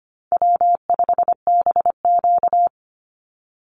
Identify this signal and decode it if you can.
Morse code, heard as a keyed tone, about 25 wpm, 700 Hz. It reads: W5BQ
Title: Call Signs Repeated 3x - 25wpm